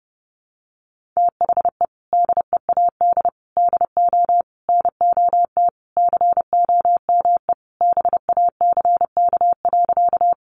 Morse code, heard as a keyed tone, about 30 wpm, 700 Hz. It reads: THE DEAD DO NOT COME BACK.
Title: Sentences from Top 1500 Words 30wpm